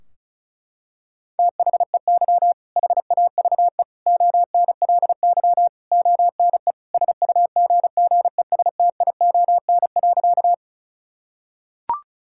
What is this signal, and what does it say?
Morse code, heard as a keyed tone, about 34 wpm, 700 Hz. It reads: THEY HAVE ONLY ONE SUGGESTION.
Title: Sentences from Top 1600 Words 35wpm